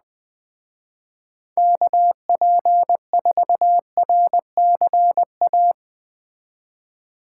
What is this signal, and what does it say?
Morse code, heard as a keyed tone, about 20 wpm, 700 Hz. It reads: KP4RCA
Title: Speed Racing - USA 2x3 Call Signs 17wpm